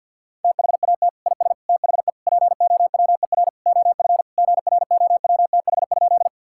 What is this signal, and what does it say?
Morse code, heard as a keyed tone, about 50 wpm, 700 Hz. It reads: THAT IS THE POWER OF GROWTH?
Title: Sentences from Top 1000 Words - Review Part 5 - 50wpm